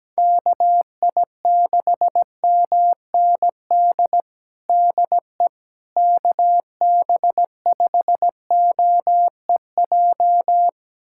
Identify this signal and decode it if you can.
Morse code, heard as a keyed tone, about 17 wpm, 700 Hz. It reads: KI6MND DE KB5OEJ
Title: USA Callsign DE Callsign 17wpm